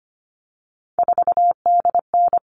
Morse code, heard as a keyed tone, about 25 wpm, 700 Hz. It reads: 4DN
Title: All Cut Numbers - 3-Digit 25wpm